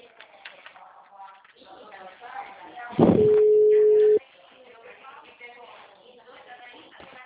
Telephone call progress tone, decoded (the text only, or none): ringback tone